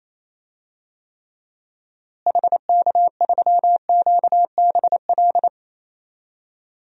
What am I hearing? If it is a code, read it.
Morse code, heard as a keyed tone, about 28 wpm, 700 Hz. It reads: HK3QBL
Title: DXCC-100 Call Signs - 6 Characters - Repeated 2x - 28wpm